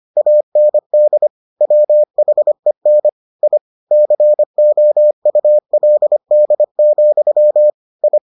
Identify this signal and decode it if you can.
Morse code, heard as a keyed tone, about 25 wpm, 600 Hz. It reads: AND WHEN I COULD, I